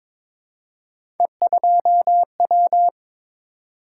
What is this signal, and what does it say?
Morse code, heard as a keyed tone, about 22 wpm, 700 Hz. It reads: E2W
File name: practice / 22wpm / Three Letter-Number Combinations - 22wpm